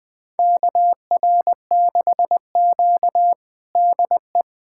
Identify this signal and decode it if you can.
Morse code, heard as a keyed tone, about 21 wpm, 700 Hz. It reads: KR6Q DE
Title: USA Callsign DE Callsign - 4 Characters - 20wpm